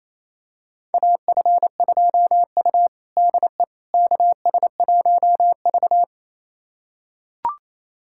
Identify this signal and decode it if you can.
Morse code, heard as a keyed tone, about 28 wpm, 700 Hz. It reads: AF2U DE KS1V
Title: USA Callsign DE Callsign - 4 Characters - 28wpm